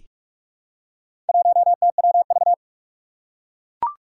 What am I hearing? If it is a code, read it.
Morse code, heard as a keyed tone, about 44 wpm, 700 Hz. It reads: 1TWV
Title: Four Challenging Letter-Number Combinations 45wpm